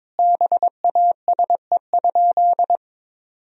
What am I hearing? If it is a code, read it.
Morse code, heard as a keyed tone, about 22 wpm, 700 Hz. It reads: BASE?
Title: Sentences from Top 1000 Words - Review Part 1 - 22wpm